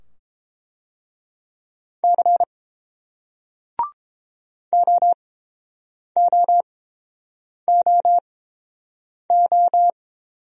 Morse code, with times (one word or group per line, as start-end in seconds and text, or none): 2.0-2.4 C
4.7-9.9 OOOO